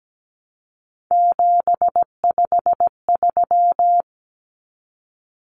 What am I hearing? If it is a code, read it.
Morse code, heard as a keyed tone, about 17 wpm, 700 Hz. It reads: 753
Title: Common Cut Numbers - 3-Digit 17wpm